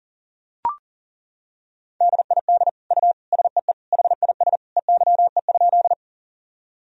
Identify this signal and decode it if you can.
Morse code, heard as a keyed tone, about 40 wpm, 700 Hz. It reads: DID U SEE HIS EYE?
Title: Sentences from Top 300 Words and Abbreviations 40wpm